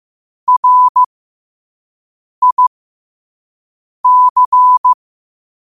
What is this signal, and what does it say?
Morse code, heard as a keyed tone, about 15 wpm, spaced as for 5 wpm, 1000 Hz. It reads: RIC
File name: qotdfiles / 5wpm